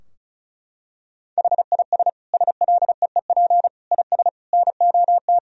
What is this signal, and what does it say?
Morse code, heard as a keyed tone, about 34 wpm, 700 Hz. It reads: HIS SLEEP IS NOT
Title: Sentences from Top 1000 Words - Review Part 4 - 35wpm